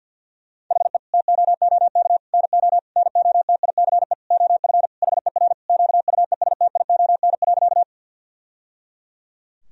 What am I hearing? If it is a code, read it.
Morse code, heard as a keyed tone, about 50 wpm, 700 Hz. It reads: HE TOOK NO NOTICE OF HER QUESTION.
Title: Sentences from Top 1000 Words - Review Part 2 - 50wpm